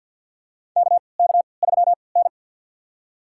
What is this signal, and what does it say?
Morse code, heard as a keyed tone, about 53 wpm, spaced as for 29 wpm, 700 Hz. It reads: KK3N
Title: USA Call Signs - 4 Characters - 3x Character Spacing - 50wpm